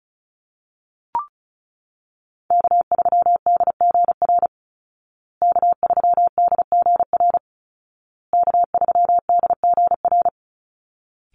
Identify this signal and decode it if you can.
Morse code, heard as a keyed tone, about 34 wpm, 700 Hz. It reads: K3DGR K3DGR K3DGR
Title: Call Signs Repeated 2x - 35wpm